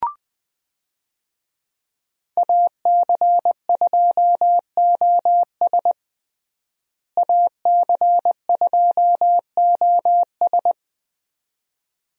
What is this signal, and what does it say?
Morse code, heard as a keyed tone, about 20 wpm, 700 Hz. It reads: AC2OS AC2OS
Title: USA Call Signs - 5 Characters - Repeated 2x - 20wpm